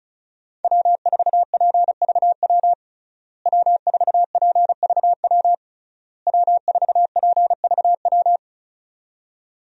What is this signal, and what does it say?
Morse code, heard as a keyed tone, about 36 wpm, 700 Hz. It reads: W4PVW W4PVW W4PVW
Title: USA Call Signs - 5 Characters - Repeated 3x - 35wpm